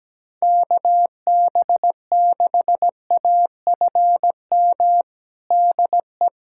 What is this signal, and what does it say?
Morse code, heard as a keyed tone, about 17 wpm, 700 Hz. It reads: KB6AFM DE
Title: USA Callsign DE Callsign - 6 Characters - 17wpm